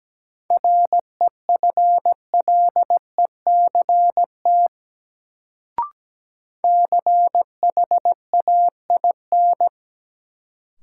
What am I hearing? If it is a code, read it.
Morse code, heard as a keyed tone, about 17 wpm, 700 Hz. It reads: REFLECT CHAIN
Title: Sentences from Top 1500 Words - New Word Drill Alt - 17wpm